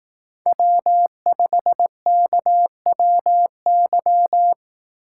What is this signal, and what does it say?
Morse code, heard as a keyed tone, about 18 wpm, 700 Hz. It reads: W5KWY
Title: Speed Racing - USA Call Signs 15wpm